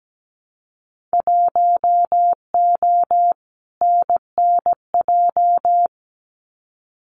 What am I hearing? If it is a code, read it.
Morse code, heard as a keyed tone, about 17 wpm, 700 Hz. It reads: 1O NNJ
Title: Winter Field Day Exchanges 17wpm